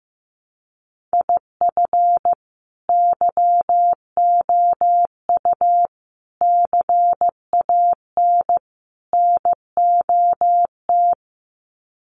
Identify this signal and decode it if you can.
Morse code, heard as a keyed tone, about 15 wpm, 700 Hz. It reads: IF YOU CAN NOT
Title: Sets of 4 Words 15wpm